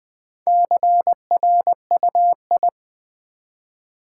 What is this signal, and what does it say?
Morse code, heard as a keyed tone, about 20 wpm, 700 Hz. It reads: CRUI
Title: Build A Word - 2 Letters At A Time - 6-Letter Words - 20wpm